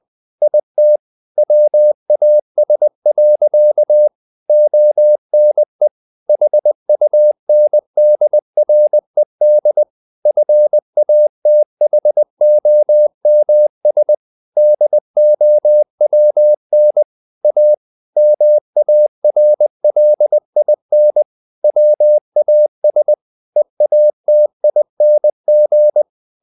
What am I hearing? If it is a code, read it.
Morse code, heard as a keyed tone, about 20 wpm, 600 Hz. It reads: IT WAS. ONE HUNDRED FATHOMS DOWN A MARLIN WAS EATING
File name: the old man and the sea / the old man and the sea part 1 20wpm